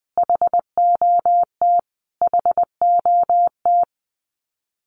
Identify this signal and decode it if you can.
Morse code, heard as a keyed tone, about 20 wpm, 700 Hz. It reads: HOT HOT
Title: Top 200 Words Repeated 20wpm